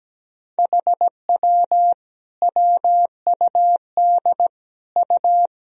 Morse code, heard as a keyed tone, about 17 wpm, 700 Hz. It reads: HW WUD U
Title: Sentences from Top 100 Words and Abbreviations 17wpm